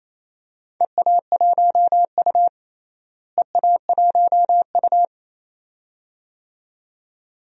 Morse code, heard as a keyed tone, about 28 wpm, 700 Hz. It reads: EA1U EA1U
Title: DXCC-100 Call Signs - 4 Characters - Repeated 2x - 28wpm